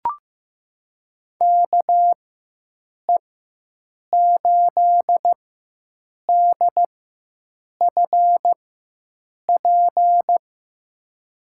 Morse code, heard as a keyed tone, about 15 wpm, spaced as for 7 wpm, 700 Hz. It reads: KE8DFP
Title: USA Call Signs - 6 Characters - 4x Character Spacing - 15wpm